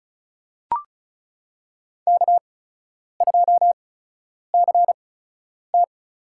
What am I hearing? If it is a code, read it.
Morse code, heard as a keyed tone, about 34 wpm, spaced as for 10 wpm, 700 Hz. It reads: K2CT
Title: USA Call Signs - 4 Characters - 8x Character Spacing - 35wpm